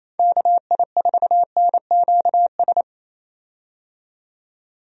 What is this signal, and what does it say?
Morse code, heard as a keyed tone, about 28 wpm, 700 Hz. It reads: KI4NQS
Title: USA Call Signs - 6 Characters - 28wpm